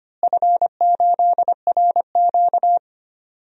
Morse code, heard as a keyed tone, about 25 wpm, 700 Hz. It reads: F8RQ
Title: DXCC-100 Call Signs - 4 Characters - Repeated 3x - 25wpm